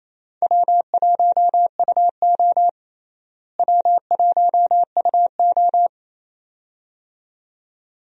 Morse code, heard as a keyed tone, about 27 wpm, 700 Hz. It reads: W1UO W1UO
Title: USA Call Signs - 4 Characters - Repeated 2x - 28wpm